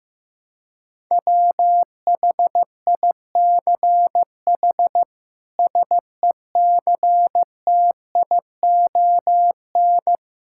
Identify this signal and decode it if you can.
Morse code, heard as a keyed tone, about 15 wpm, 700 Hz. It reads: WHICH SECTION
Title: Sentences from Top 1000 Words - Review Part 5 - 15wpm